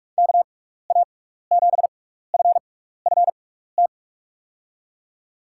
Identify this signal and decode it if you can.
Morse code, heard as a keyed tone, about 44 wpm, spaced as for 15 wpm, 700 Hz. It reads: KA7FFT
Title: USA Call Signs - 6x Character Spacing - 45wpm